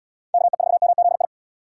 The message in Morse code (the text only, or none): KE4MZI